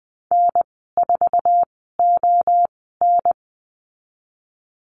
Morse code, heard as a keyed tone, about 20 wpm, spaced as for 15 wpm, 700 Hz. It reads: N4ON
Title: 048 Lesson 9g - Number 4 - Call Signs 20wpm